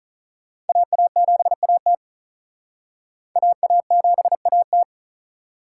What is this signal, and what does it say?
Morse code, heard as a keyed tone, about 47 wpm, 700 Hz. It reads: AA7AT AA7AT